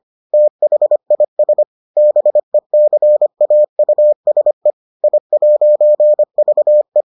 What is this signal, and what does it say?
Morse code, heard as a keyed tone, about 25 wpm, 600 Hz. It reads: THIS BECAUSE I'VE